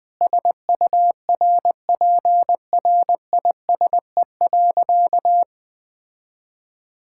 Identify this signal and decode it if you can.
Morse code, heard as a keyed tone, about 20 wpm, 700 Hz. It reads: SURPRISE.
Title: Sentences with 4-Word Phrases - Spoken Beginning - 20wpm